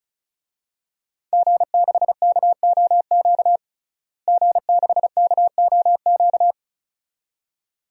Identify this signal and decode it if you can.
Morse code, heard as a keyed tone, about 34 wpm, 700 Hz. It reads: G6KOQ G6KOQ
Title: DXCC-100 Call Signs - 5 Characters - Repeated 2x - 35wpm